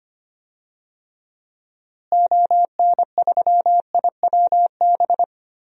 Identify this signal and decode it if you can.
Morse code, heard as a keyed tone, about 25 wpm, 700 Hz. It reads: ON3IWB
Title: DXCC-100 Call Signs - 6 Characters - Repeated 3x - 25wpm